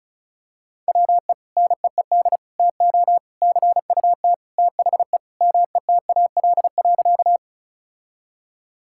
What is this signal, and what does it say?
Morse code, heard as a keyed tone, about 34 wpm, 700 Hz. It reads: WE NEED TO CUT THE METAL.
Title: Sentences from Top 1000 Words - Review Part 6 - 35wpm